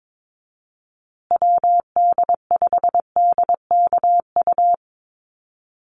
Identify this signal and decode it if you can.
Morse code, heard as a keyed tone, about 22 wpm, 700 Hz. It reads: WD5DKU
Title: USA Call Signs Repeated 3x 22wpm